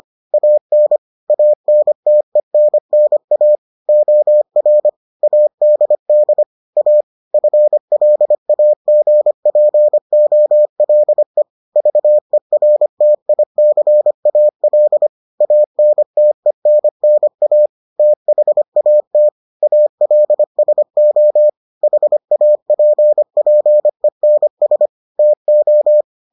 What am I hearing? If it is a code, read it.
Morse code, heard as a keyed tone, about 25 wpm, 600 Hz. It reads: AN ANTENNA OR ADD A FLAGPOLE VERTICAL ANTENNA THAT ALSO HAPPENS TO